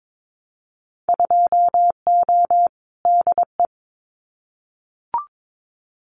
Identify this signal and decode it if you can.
Morse code, heard as a keyed tone, about 22 wpm, 700 Hz. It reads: 2O DE
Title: Winter Field Day Exchanges Encore 22wpm